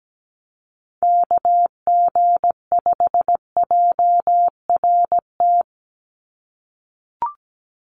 Morse code, heard as a keyed tone, about 17 wpm, 700 Hz. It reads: KG5JRT
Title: USA Call Signs 17wpm